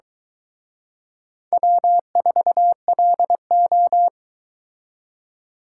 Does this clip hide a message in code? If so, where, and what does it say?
Yes: Morse code from 1.5 to 4.1 s, heard as a keyed tone, about 22 wpm, 700 Hz. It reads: W4LO